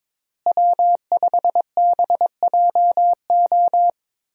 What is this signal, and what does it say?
Morse code, heard as a keyed tone, about 22 wpm, 700 Hz. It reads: W5BJO
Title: USA Call Signs - 5 Characters - Repeated 3x - 22wpm